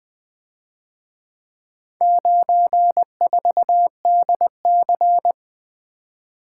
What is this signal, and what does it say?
Morse code, heard as a keyed tone, about 24 wpm, 700 Hz. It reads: 94DC
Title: Speed-Racing - Four Letter-Number Combinations - 20wpm